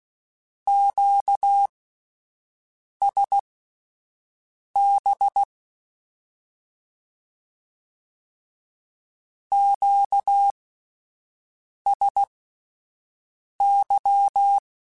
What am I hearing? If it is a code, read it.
Morse code, heard as a keyed tone, about 16 wpm, spaced as for 5 wpm, 800 Hz. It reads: QSB QSY
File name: (spaced) / Q-Calls Spaced-mp3s / Q Calls Twice sp @10